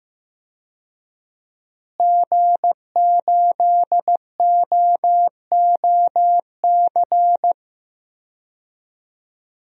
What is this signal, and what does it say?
Morse code, heard as a keyed tone, about 15 wpm, 700 Hz. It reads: G8OOC
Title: DXCC-100 Call Signs - 5 Characters - Repeated 3x - 15wpm